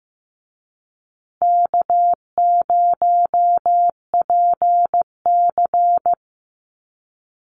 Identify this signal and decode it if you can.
Morse code, heard as a keyed tone, about 15 wpm, 700 Hz. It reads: K0PC
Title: Easy Call Signs Repeated 2x - 15wpm